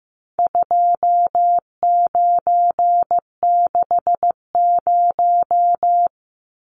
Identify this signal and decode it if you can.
Morse code, heard as a keyed tone, about 15 wpm, 700 Hz. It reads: 2960
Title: Numbers - 4-Digit 15wpm